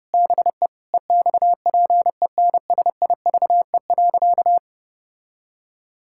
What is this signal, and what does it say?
Morse code, heard as a keyed tone, about 32 wpm, 700 Hz. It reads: BE EXPENSIVE.
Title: Sentences from Top 1000 Words - Review Part 6 - 30wpm